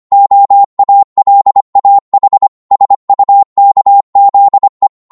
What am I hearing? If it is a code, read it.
Morse code, heard as a keyed tone, about 25 wpm, 800 Hz. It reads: OALAH SUKZE
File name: Morse / ltr-25wpm